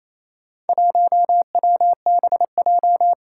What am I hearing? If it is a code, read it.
Morse code, heard as a keyed tone, about 27 wpm, 700 Hz. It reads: 1WBJ
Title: Four Challenging Letter-Number Combinations 28wpm